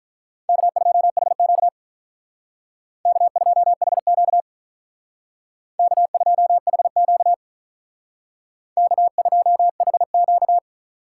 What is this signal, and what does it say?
Morse code, heard as a keyed tone, about 45 wpm, 700 Hz. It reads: K2HQ K2HQ K2HQ K2HQ